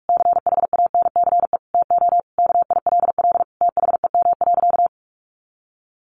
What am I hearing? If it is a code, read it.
Morse code, heard as a keyed tone, about 44 wpm, 700 Hz. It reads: CHANCE TO KILL THEM.
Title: Sentences from Top 1400 Words 45wpm